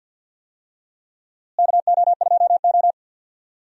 Morse code, heard as a keyed tone, about 46 wpm, 700 Hz. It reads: KO2O
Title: USA Call Signs - 4 Characters - Repeated 2x - 50wpm